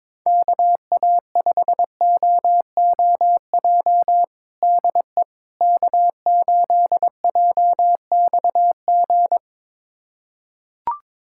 Morse code, heard as a keyed tone, about 22 wpm, 700 Hz. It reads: KA5OOJ DE K8JXG
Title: USA Callsign DE Callsign 22wpm